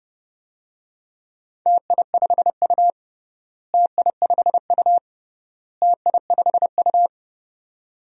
Morse code, heard as a keyed tone, about 30 wpm, 700 Hz. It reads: TI5U TI5U TI5U
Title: DXCC-100 Call Signs - 4 Characters - Repeated 3x - 30wpm